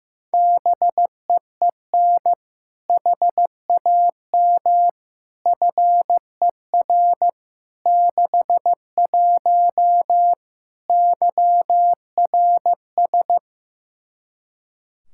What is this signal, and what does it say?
Morse code, heard as a keyed tone, about 15 wpm, 700 Hz. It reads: BEEN HAM FER 61 YRS
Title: K1USN SST QSO Practice 15wpm